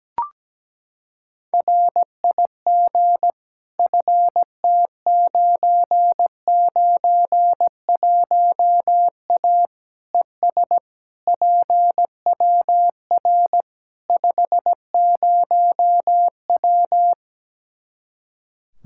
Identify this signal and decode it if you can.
Morse code, heard as a keyed tone, about 17 wpm, 700 Hz. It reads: RIG FT991A ES PWR 50W
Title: Practice QSO - Exchange 2 - Info - 17wpm